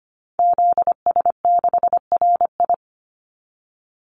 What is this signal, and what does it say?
Morse code, heard as a keyed tone, about 25 wpm, 700 Hz. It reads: ZS6RI
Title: Call Signs Repeated 3x - 25wpm